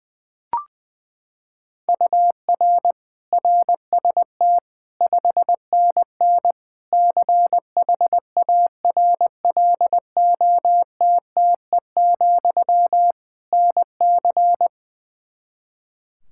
Morse code, heard as a keyed tone, about 20 wpm, 700 Hz. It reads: UR RST 5NN CHARLOTTE, NC
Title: K1USN SST QSO Practice 20wpm